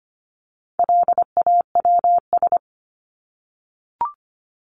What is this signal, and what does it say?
Morse code, heard as a keyed tone, about 25 wpm, 700 Hz. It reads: LAWS
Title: ES:S Suffix - Top 100 Words with Base Words 25wpm